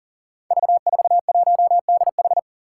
Morse code, heard as a keyed tone, about 37 wpm, 700 Hz. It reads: V41DH
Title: Five Challenging Letter-Number Combinations 40wpm